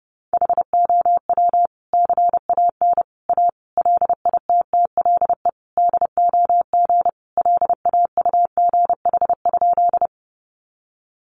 Morse code, heard as a keyed tone, about 30 wpm, 700 Hz. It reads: HOW CAN A LITTLE DOG LAUGH?
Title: Sentences from Top 500 Words 30wpm